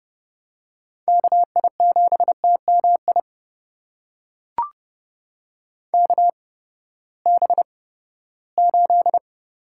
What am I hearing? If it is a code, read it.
Morse code, heard as a keyed tone, about 29 wpm, spaced as for 17 wpm, 700 Hz. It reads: KI7TMI KB8
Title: USA Call Signs - 6 Characters - 8x Character Spacing - 30wpm